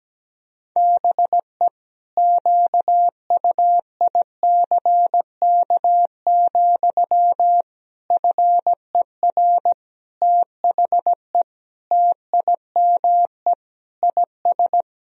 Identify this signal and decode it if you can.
Morse code, heard as a keyed tone, about 17 wpm, 700 Hz. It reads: BE QUICK, FER THE TIME IS
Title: Sentences from Top 500 Words and Abbreviations 17wpm